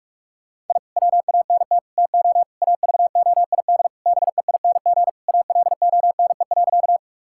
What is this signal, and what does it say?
Morse code, heard as a keyed tone, about 41 wpm, 700 Hz. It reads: I WANT TO AVOID BEING ALONE.
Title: Sentences from Top 1000 Words - Review Part 6 - 45wpm